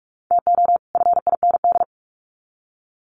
Morse code, heard as a keyed tone, about 44 wpm, 700 Hz. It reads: TO FIND
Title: Sets of 2 Words 45wpm